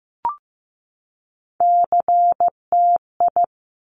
Morse code, heard as a keyed tone, about 15 wpm, 700 Hz. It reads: CTI
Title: Top Three Letter Combinations in Words 15wpm